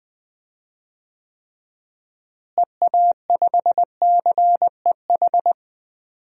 Morse code, heard as a keyed tone, about 21 wpm, 700 Hz. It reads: EA5CEH
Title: DXCC-100 Call Signs - 6 Characters - Repeated 2x - 20wpm